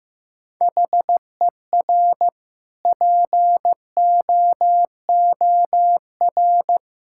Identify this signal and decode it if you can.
Morse code, heard as a keyed tone, about 16 wpm, 700 Hz. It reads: HER POOR
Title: Sentences from Top 1000 Words - Review Part 6 - 15wpm